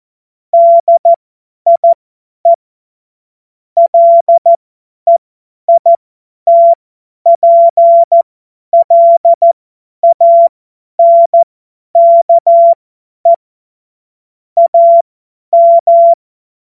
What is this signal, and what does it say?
Morse code, heard as a keyed tone, about 14 wpm, spaced as for 10 wpm, 680 Hz. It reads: DIE LEITPLANKE AM
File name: Morsekurs / Witze / Witz 30. Dezember 2025 50 F70